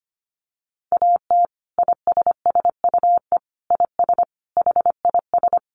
Morse code, heard as a keyed tone, about 25 wpm, 700 Hz. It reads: AT ISSUE IS HIS
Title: Sentences from Top 900 Words 25wpm